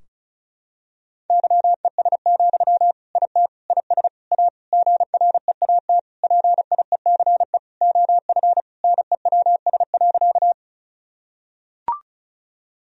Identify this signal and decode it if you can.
Morse code, heard as a keyed tone, about 36 wpm, 700 Hz. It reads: YES, IT IS A GREAT PIECE OF NEWS.
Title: Sentences from Top 1000 Words - Review Part 2 - 35wpm